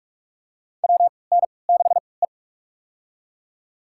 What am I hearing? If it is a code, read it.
Morse code, heard as a keyed tone, about 47 wpm, spaced as for 26 wpm, 700 Hz. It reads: WN6E
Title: USA Call Signs - 4 Characters - 3x Character Spacing - 45wpm